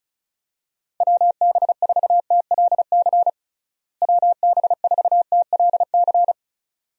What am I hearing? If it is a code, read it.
Morse code, heard as a keyed tone, about 36 wpm, 700 Hz. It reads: WB4TLC WB4TLC
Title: USA Call Signs - 6 Characters - Repeated 2x - 35wpm